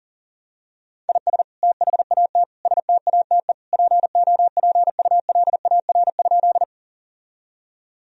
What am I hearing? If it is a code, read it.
Morse code, heard as a keyed tone, about 37 wpm, 700 Hz. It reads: IS THAT STATE POPULAR?
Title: Sentences from Top 1000 Words - Review Part 3 - 40wpm